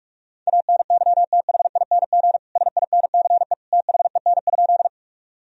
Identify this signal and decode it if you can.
Morse code, heard as a keyed tone, about 46 wpm, 700 Hz. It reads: ANYTHING SINCE THEN?
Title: Sentences from Top 1000 Words - Review Part 1 - 45wpm